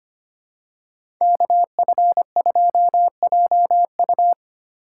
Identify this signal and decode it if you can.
Morse code, heard as a keyed tone, about 25 wpm, 700 Hz. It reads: KF2JU
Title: USA Call Signs - 5 Characters - 3x Character Spacing - 25wpm